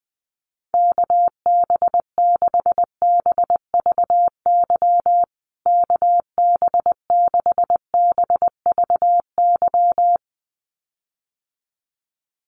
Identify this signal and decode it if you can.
Morse code, heard as a keyed tone, about 20 wpm, 700 Hz. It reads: KB6BVY KB6BVY
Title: USA Call Signs Repeated 2x 20wpm